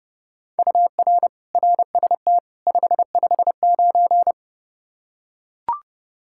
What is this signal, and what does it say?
Morse code, heard as a keyed tone, about 30 wpm, 700 Hz. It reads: UR RST 559
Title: Practice QSO - Exchange 1 - Beginning - 30wpm